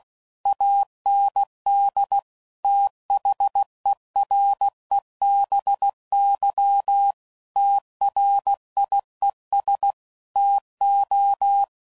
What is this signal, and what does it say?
Morse code, heard as a keyed tone, about 16 wpm, 800 Hz. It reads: AND THEREBY TRIES TO
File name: CW Code Groups @15WPM / Technical Subjects / XL